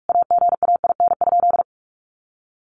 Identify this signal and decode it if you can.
Morse code, heard as a keyed tone, about 44 wpm, 700 Hz. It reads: AGAIN?
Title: Sentences from Top 200 Words 45wpm